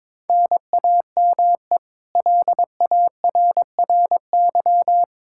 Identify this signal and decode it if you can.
Morse code, heard as a keyed tone, about 22 wpm, 700 Hz. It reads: NAME LARRY
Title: Practice QSO - Exchange 1 - Beginning - 22wpm